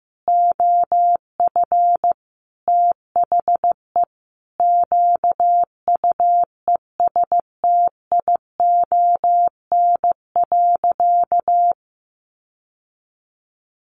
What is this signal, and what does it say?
Morse code, heard as a keyed tone, about 15 wpm, 700 Hz. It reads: OF THE QUESTION.
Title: Sentences from Top 1500 Words 15wpm